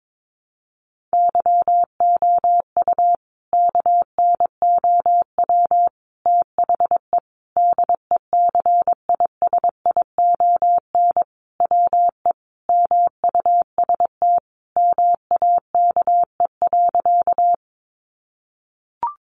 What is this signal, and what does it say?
Morse code, heard as a keyed tone, about 22 wpm, 700 Hz. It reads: YOU KNOW THE DECISION WE MUST MAKE.
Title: Sentences from Top 1000 Words 22wpm